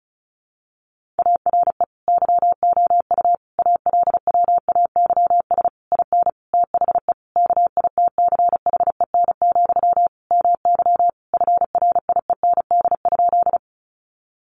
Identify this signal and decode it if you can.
Morse code, heard as a keyed tone, about 34 wpm, 700 Hz. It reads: ARE YOU ALWAYS IN THE KITCHEN, MY FRIEND?
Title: Sentences from Top 1400 Words 35wpm